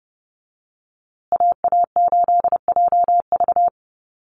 Morse code, heard as a keyed tone, about 30 wpm, 700 Hz. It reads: AA8JV
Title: USA Call Signs Repeated 2x 30wpm